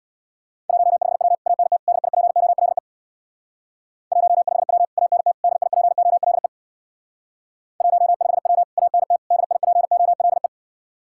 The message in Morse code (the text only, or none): 15W ANT DIPOLE 15W ANT DIPOLE 15W ANT DIPOLE